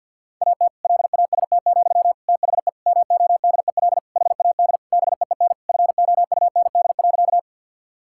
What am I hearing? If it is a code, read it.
Morse code, heard as a keyed tone, about 51 wpm, 700 Hz. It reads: AT LAST, THE MODEL HAD BEEN FOUND.
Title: Sentences from Top 1000 Words - Review Part 5 - 50wpm